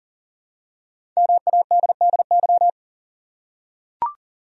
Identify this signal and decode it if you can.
Morse code, heard as a keyed tone, about 40 wpm, 700 Hz. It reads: MADDY
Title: USA Common Names - Max 5 Letters - 40wpm